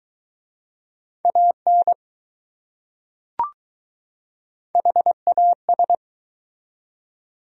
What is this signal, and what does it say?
Morse code, heard as a keyed tone, about 20 wpm, 700 Hz. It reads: AN HAS